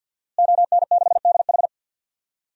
Morse code, heard as a keyed tone, about 52 wpm, 700 Hz. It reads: ON6DH
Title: DXCC-100 Call Signs - 5 Characters - Repeated 3x - 50wpm